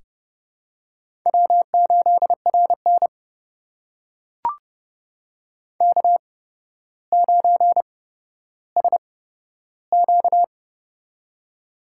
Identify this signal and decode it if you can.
Morse code, heard as a keyed tone, about 30 wpm, spaced as for 13 wpm, 700 Hz. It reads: W8RN K9SQ
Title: USA Call Signs - 4 Characters - 8x Character Spacing - 30wpm